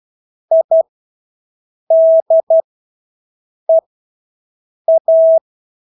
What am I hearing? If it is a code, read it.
Morse code, heard as a keyed tone, about 12 wpm, spaced as for 6 wpm, 650 Hz. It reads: IDEA
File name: Engelske ord 12-6 wpm